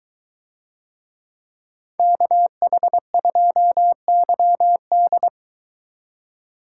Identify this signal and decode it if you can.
Morse code, heard as a keyed tone, about 23 wpm, 700 Hz. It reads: KH2YD